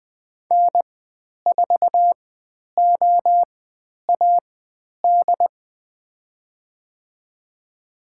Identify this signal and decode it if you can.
Morse code, heard as a keyed tone, about 20 wpm, spaced as for 10 wpm, 700 Hz. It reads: N4OAD
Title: 102 Lesson 17g - Letter u - Call Signs 20wpm